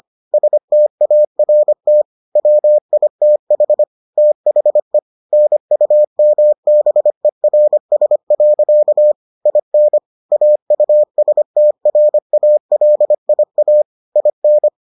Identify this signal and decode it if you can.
Morse code, heard as a keyed tone, about 25 wpm, 600 Hz. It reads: START WITH THE NUMBERS. IN AUSTRALIA IN